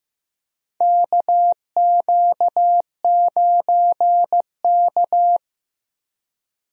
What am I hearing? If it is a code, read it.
Morse code, heard as a keyed tone, about 15 wpm, 700 Hz. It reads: KQ9K
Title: USA Call Signs - 4 Characters - Repeated 2x - 15wpm